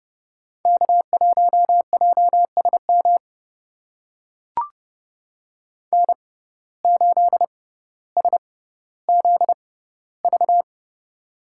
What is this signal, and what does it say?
Morse code, heard as a keyed tone, about 31 wpm, spaced as for 15 wpm, 700 Hz. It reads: K1JSM N8SZV
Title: USA Call Signs - 5 Characters - 6x Character Spacing - 30wpm